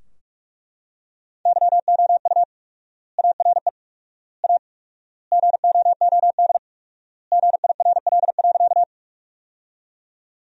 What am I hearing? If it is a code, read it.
Morse code, heard as a keyed tone, about 41 wpm, 700 Hz. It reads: YOU ARE A GOOD GIRL.
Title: 4x Word-Spacing - Sentences from Top 400 Words - 45wpm